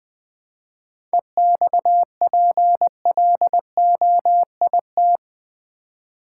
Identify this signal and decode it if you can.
Morse code, heard as a keyed tone, about 20 wpm, 700 Hz. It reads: EXPLOIT
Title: 7 Letter Words - First 3 Letters Given - 20wpm